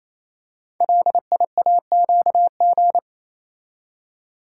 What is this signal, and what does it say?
Morse code, heard as a keyed tone, about 27 wpm, 700 Hz. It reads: LIAQG
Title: Code Groups - 5 Letters - 28wpm